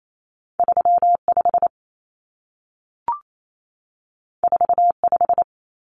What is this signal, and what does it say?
Morse code, heard as a keyed tone, about 28 wpm, 700 Hz. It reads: 35 45
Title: Numbers - 2-Digit 28wpm